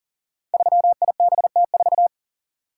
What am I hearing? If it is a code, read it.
Morse code, heard as a keyed tone, about 40 wpm, 700 Hz. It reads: 3IBT4
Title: Code Groups - 5 Letter:Numbers - 40wpm